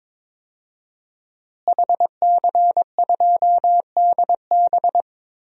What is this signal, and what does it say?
Morse code, heard as a keyed tone, about 22 wpm, 700 Hz. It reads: HC2DB
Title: DXCC-100 Call Signs - 5 Characters - Repeated 2x - 22wpm